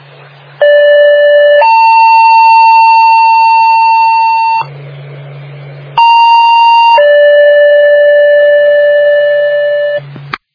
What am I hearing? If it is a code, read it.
Two-tone page, heard as a one-second tone, then a three-second tone, about 580 then 900 Hz, about 930 then 580 Hz. It is heard from 0.6 to 10.0 s.